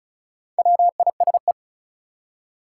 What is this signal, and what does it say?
Morse code, heard as a keyed tone, about 34 wpm, 700 Hz. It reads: WISE
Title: Four Letter Words - 35wpm